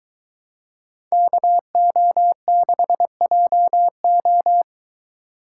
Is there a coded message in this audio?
Yes: Morse code from 1.1 to 4.6 s, heard as a keyed tone, about 23 wpm, 700 Hz. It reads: KO6JO